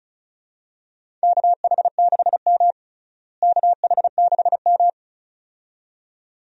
Morse code, heard as a keyed tone, about 36 wpm, 700 Hz. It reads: KH6M KH6M
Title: USA Call Signs - 4 Characters - Repeated 2x - 35wpm